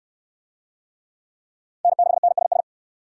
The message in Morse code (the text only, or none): N6NSS